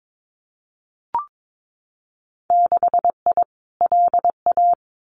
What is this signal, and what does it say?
Morse code, heard as a keyed tone, about 22 wpm, 700 Hz. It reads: 6I LA
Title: Winter Field Day Exchanges Encore 22wpm